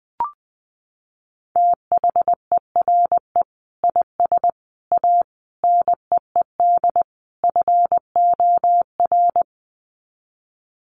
Morse code, heard as a keyed tone, about 20 wpm, 700 Hz. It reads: THERE IS A NEED FOR
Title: Sets of 5 Words Encore2 20wpm